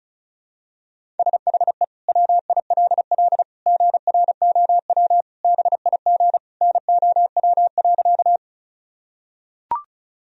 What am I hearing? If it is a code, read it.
Morse code, heard as a keyed tone, about 36 wpm, 700 Hz. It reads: SHE WILL GROW BIG NOW.
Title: Sentences from Top 1000 Words - Review Part 1 - 35wpm